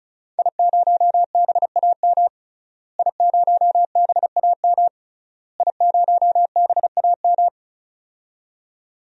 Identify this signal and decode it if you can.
Morse code, heard as a keyed tone, about 36 wpm, 700 Hz. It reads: I0BAM I0BAM I0BAM
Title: DXCC-100 Call Signs - 5 Characters - Repeated 3x - 35wpm